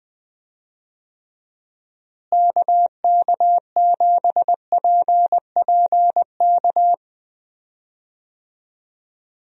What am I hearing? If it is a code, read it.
Morse code, heard as a keyed tone, about 21 wpm, 700 Hz. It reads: KK7PPK
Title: USA Call Signs - 6 Characters - Repeated 2x - 20wpm